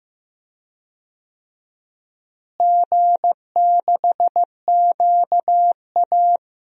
Morse code, heard as a keyed tone, about 15 wpm, 700 Hz. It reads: G6QA
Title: DXCC-100 Call Signs - 4 Characters - Repeated 3x - 15wpm